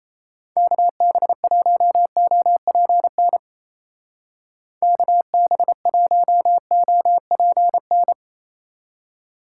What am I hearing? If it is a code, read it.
Morse code, heard as a keyed tone, about 37 wpm, 700 Hz. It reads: KB1OPN KB1OPN